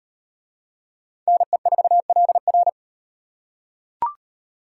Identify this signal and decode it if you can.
Morse code, heard as a keyed tone, about 38 wpm, 700 Hz. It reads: NE4LR